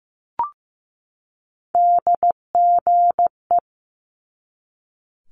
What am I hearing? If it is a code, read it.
Morse code, heard as a keyed tone, about 15 wpm, 700 Hz. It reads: DGE
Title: Multi-letter Phonemes 15wpm